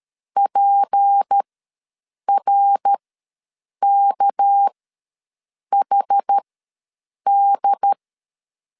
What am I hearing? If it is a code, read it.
Morse code, heard as a keyed tone, about 13 wpm, spaced as for 7 wpm, 800 Hz. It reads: PRKHD
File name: LEZIONI / 6 PKX0 / 6 PK ws 12wpm 3116